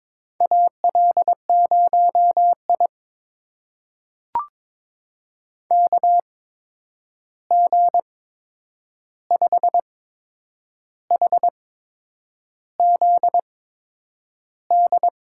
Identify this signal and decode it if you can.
Morse code, heard as a keyed tone, about 22 wpm, spaced as for 8 wpm, 700 Hz. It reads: AL0I KG5HZD
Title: USA Call Signs - 8x Character Spacing - 22wpm